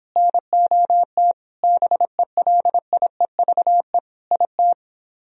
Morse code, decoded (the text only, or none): NOT BELIEVE IT